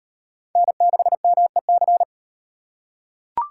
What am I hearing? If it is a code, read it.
Morse code, heard as a keyed tone, about 38 wpm, 700 Hz. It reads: N6MEC